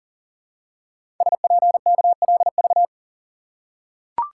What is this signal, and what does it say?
Morse code, heard as a keyed tone, about 38 wpm, 700 Hz. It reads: SPKLV
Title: Code Groups - 5 Letters - 40wpm